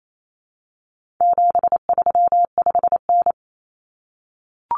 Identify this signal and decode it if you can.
Morse code, heard as a keyed tone, about 28 wpm, 700 Hz. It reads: 735N
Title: Common Cut Numbers - 4-Digit 28wpm